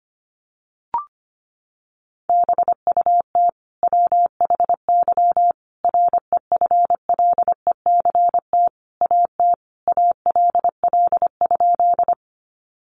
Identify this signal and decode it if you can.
Morse code, heard as a keyed tone, about 25 wpm, 700 Hz. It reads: BUT WHY REFLECT AT ALL?
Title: Sentences from Top 1500 Words 25wpm